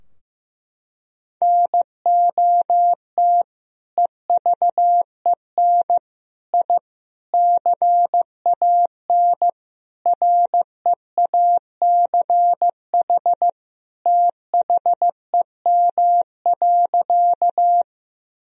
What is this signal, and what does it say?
Morse code, heard as a keyed tone, about 15 wpm, 700 Hz. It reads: NOT EVEN I CAN REACH THEM.
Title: Sentences from Top 1000 Words - Review Part 2 - 15wpm